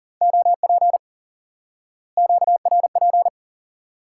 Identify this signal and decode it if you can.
Morse code, heard as a keyed tone, about 40 wpm, 700 Hz. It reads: OP QRP
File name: IQE Course 3 - Lesson 89 - YAGI - Two Elements - 40wpm